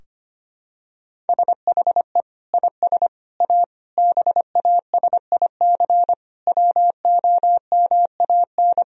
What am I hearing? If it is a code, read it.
Morse code, heard as a keyed tone, about 25 wpm, 700 Hz. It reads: SHE IS A BASIC WOMAN
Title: Sentences from Top 1000 Words - Review Part 3 - 25wpm